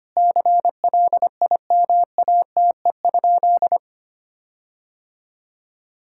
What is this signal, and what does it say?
Morse code, heard as a keyed tone, about 25 wpm, 700 Hz. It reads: CLIMATE?
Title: Sentences from Top 1600 Words 25wpm